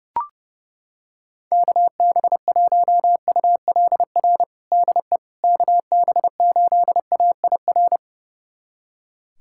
Morse code, heard as a keyed tone, about 30 wpm, 700 Hz. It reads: KB1ULR DE KB8AIR
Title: USA Callsign DE Callsign 30wpm